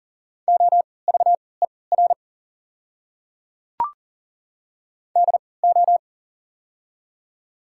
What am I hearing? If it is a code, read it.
Morse code, heard as a keyed tone, about 40 wpm, spaced as for 23 wpm, 690 Hz. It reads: OVER DO
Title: 3x Character Spacing - Top 100 Words - 40wpm